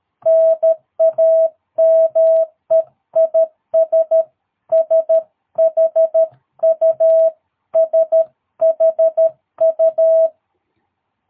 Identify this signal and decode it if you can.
Morse code, heard as a keyed tone, about 13 wpm, 660 Hz. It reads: NAME IS SHU SHU
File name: jh8cbh / 2022cw